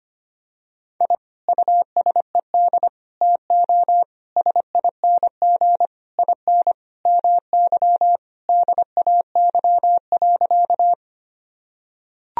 Morse code, heard as a keyed tone, about 25 wpm, 700 Hz. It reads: I USED TO SING IN MY DAY.
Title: Sentences from Top 1000 Words - Review Part 2 - 25wpm